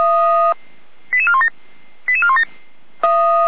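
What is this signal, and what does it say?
Five-tone selective call (ZVEI), heard as quick runs of five tones, tones of about 70 ms. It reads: 88417 88417
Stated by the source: ZVEI five-tone call received by SDR, Feuerwehr Orlamünde fire brigade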